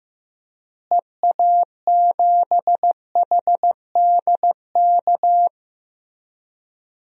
Morse code, heard as a keyed tone, about 15 wpm, 700 Hz. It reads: EA7HDK
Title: DXCC-100 Call Signs - 6 Characters - Repeated 3x - 15wpm